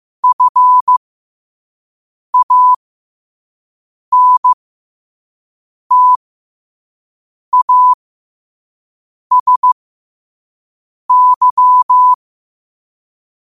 Morse code, heard as a keyed tone, about 15 wpm, spaced as for 5 wpm, 1000 Hz. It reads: FANTASY
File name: qotdfiles / 5wpm